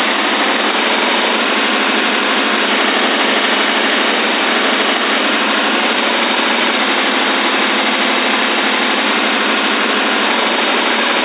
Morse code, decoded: none